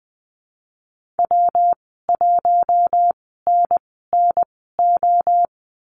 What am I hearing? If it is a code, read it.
Morse code, heard as a keyed tone, about 20 wpm, spaced as for 14 wpm, 700 Hz. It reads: W1NNO
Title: 122 Lesson 20g - Number 3 - Call Signs 20wpm